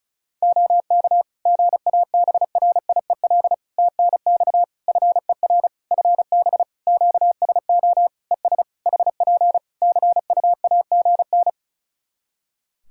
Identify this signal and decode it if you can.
Morse code, heard as a keyed tone, about 36 wpm, 700 Hz. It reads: OK GABRIEL TNX FER FB QSO ES HP CUAGN
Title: Practice QSO - Exchange 4 - Ending - 35wpm